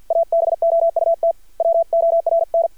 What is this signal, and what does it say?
Morse code, heard as a keyed tone, about 50 wpm, 670 Hz. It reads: ABOUT WORN
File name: archived / CRSnet